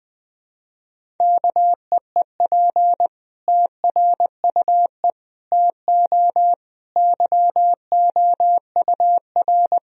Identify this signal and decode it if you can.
Morse code, heard as a keyed tone, about 21 wpm, 700 Hz. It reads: KEEP TRUE TO YOUR
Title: Sentences from Top 1000 Words - Review Part 3 - 20wpm